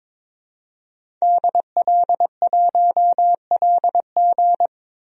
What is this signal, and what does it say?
Morse code, heard as a keyed tone, about 22 wpm, 700 Hz. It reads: DL1LG
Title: DXCC-100 Call Signs - 5 Characters - 22wpm